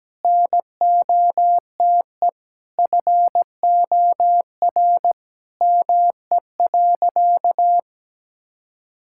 Morse code, heard as a keyed tone, about 17 wpm, 700 Hz. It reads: NOTE FOR ME.
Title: Sentences from Top 1000 Words - Review Part 6 - 17wpm